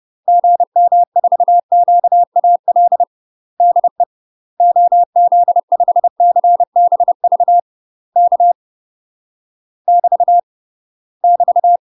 Morse code, heard as a keyed tone, about 30 wpm, 700 Hz. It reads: GM4QAL DE OZ5CBV K <BT> <BT>
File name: UK MASTER QSOs TRAINING / 30WPM / UK MASTER QSOs 0039